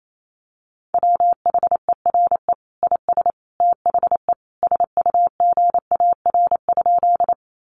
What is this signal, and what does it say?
Morse code, heard as a keyed tone, about 28 wpm, 700 Hz. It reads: WHERE IS THE SUGAR?
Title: Sentences from Top 1200 Words 28wpm